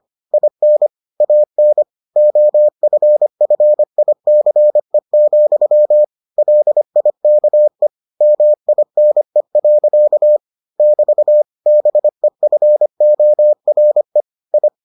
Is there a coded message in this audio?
Yes: Morse code from 0.3 to 14.7 s, heard as a keyed tone, about 25 wpm, 600 Hz. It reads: IN AN OFFICE, LIKE MINE. <BT> BEFORE I